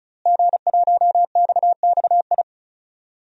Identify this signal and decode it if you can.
Morse code, heard as a keyed tone, about 36 wpm, 700 Hz. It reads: G1XXI
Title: DXCC-100 Call Signs - 5 Characters - Repeated 2x - 35wpm